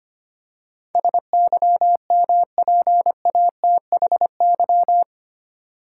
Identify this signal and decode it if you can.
Morse code, heard as a keyed tone, about 25 wpm, 700 Hz. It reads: SYMPATHY
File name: 8 Letter Words - First 3 Letters Given - 25wpm